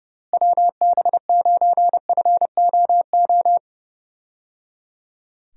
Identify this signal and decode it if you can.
Morse code, heard as a keyed tone, about 31 wpm, 700 Hz. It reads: WB9FOO
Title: USA Call Signs - 6 Characters - Repeated 3x - 30wpm